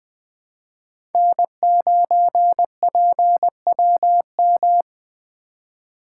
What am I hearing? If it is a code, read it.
Morse code, heard as a keyed tone, about 21 wpm, 700 Hz. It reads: N9PWM
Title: USA Call Signs - 5 Characters - 2x Character Spacing - 20wpm